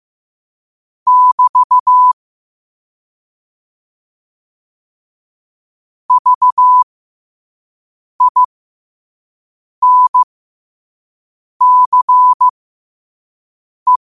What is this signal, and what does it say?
Morse code, heard as a keyed tone, about 15 wpm, spaced as for 5 wpm, 1000 Hz. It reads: <BT> VINCE
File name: qotdfiles / 5wpm